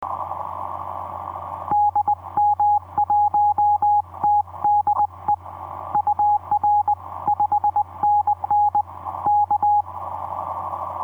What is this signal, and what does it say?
Morse code, heard as a keyed tone, about 20 wpm, 830 Hz. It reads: DM1TDE UR 5NN K
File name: v5-dk1ce dm1tde etc 17m cw 28dec2010 0942